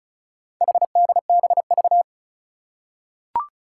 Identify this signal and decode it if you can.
Morse code, heard as a keyed tone, about 36 wpm, 700 Hz. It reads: HDBV
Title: Four Challenging Letter-Number Combinations 35wpm